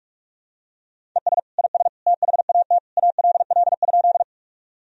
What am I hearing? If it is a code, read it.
Morse code, heard as a keyed tone, about 46 wpm, 700 Hz. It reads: ES IS THAT ALL?
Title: Sentences from Top 100 Words and Abbreviations 45wpm